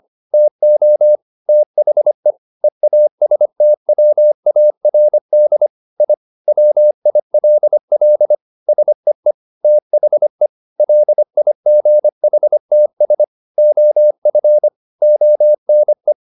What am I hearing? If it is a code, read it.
Morse code, heard as a keyed tone, about 25 wpm, 600 Hz. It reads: TO THE EASTWARD I WILL SEE THE LIGHTS OF ONE